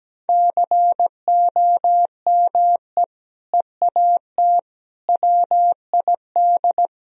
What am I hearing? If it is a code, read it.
Morse code, heard as a keyed tone, about 17 wpm, 700 Hz. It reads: COME EAT WID
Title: Sentences from Top 300 Words and Abbreviations 17wpm